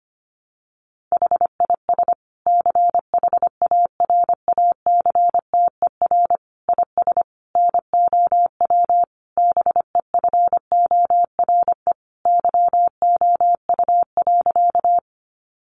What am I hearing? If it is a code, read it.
Morse code, heard as a keyed tone, about 25 wpm, 700 Hz. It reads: HIS CHARACTER IS NOW BEFORE YOU.
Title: Sentences from Top 1200 Words 25wpm